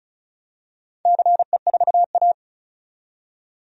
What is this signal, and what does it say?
Morse code, heard as a keyed tone, about 35 wpm, 700 Hz. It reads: CE4A
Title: DXCC-100 Call Signs - 4 Characters - 35wpm